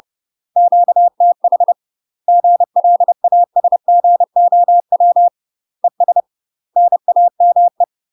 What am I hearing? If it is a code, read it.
Morse code, heard as a keyed tone, about 30 wpm, 700 Hz. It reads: QTH GLASGOW ES NAME
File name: UK MASTER QSOs TRAINING / 30WPM / UK MASTER QSOs 0019